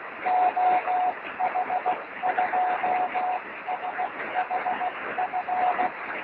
Morse code, heard as a keyed tone, about 16 wpm, 730 Hz, with noise. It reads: OH2SHF
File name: sm7lcb / oh2shf 23cm